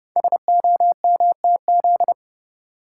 Morse code, heard as a keyed tone, about 30 wpm, 700 Hz. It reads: SOMTZ
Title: Code Groups - 5 Letter:Numbers - 30wpm